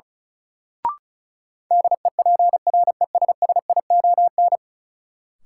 Morse code, heard as a keyed tone, about 36 wpm, 700 Hz. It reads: DEPRESSION
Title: Sentences from Top 1300 Words - New Word Drill - 35wpm